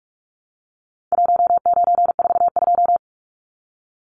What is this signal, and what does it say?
Morse code, heard as a keyed tone, about 44 wpm, 700 Hz. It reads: 1942
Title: Numbers - Contest Style - 4-Digit 45wpm